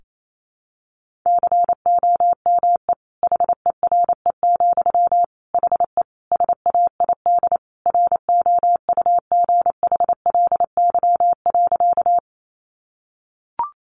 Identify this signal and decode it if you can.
Morse code, heard as a keyed tone, about 28 wpm, 700 Hz. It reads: COME HERE, HE SAID ROUGHLY.
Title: Sentences from Top 1500 Words 28wpm